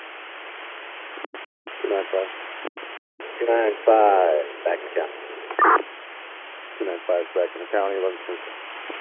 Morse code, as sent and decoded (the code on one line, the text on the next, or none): none